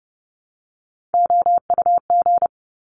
Morse code, heard as a keyed tone, about 30 wpm, 700 Hz. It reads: OUG
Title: Top Three Letter Combinations in Words 30wpm